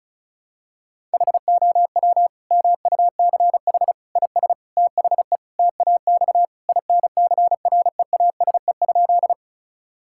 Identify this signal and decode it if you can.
Morse code, heard as a keyed tone, about 36 wpm, 700 Hz. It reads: HOW MUCH IS THE TAX INCREASE?
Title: Sentences from Top 1000 Words - Review Part 5 - 35wpm